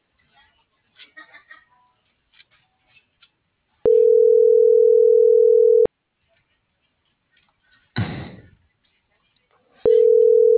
Ringback tone (North America), 440 + 480 Hz.